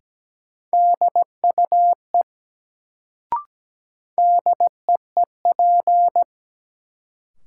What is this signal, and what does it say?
Morse code, heard as a keyed tone, about 17 wpm, 700 Hz. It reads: DUE DEEP
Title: Sets of 5 Words Encore - New Word Drill - 17wpm